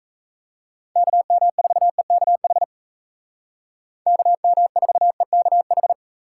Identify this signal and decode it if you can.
Morse code, heard as a keyed tone, about 36 wpm, 700 Hz. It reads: KM4EKH KM4EKH